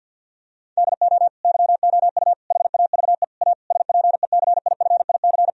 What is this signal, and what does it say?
Morse code, heard as a keyed tone, about 50 wpm, 700 Hz. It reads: DO YOU HAVE A SPECIFIC
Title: Sentences from Top 1000 Words - Review Part 3 - 50wpm